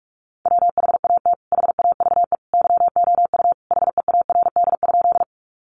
Morse code, heard as a keyed tone, about 44 wpm, 700 Hz. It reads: WHAT HAVE YOU HEARD?
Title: Sentences from Top 400 Words 45wpm